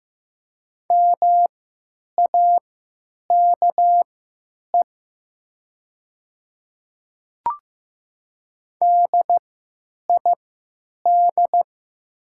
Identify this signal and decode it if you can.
Morse code, heard as a keyed tone, about 15 wpm, spaced as for 9 wpm, 700 Hz. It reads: MAKE DID
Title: 3x Character Spacing - Top 200 Words - 15wpm